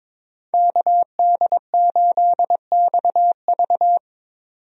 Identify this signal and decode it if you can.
Morse code, heard as a keyed tone, about 22 wpm, 700 Hz. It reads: KD8XV
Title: USA Call Signs - 5 Characters - 22wpm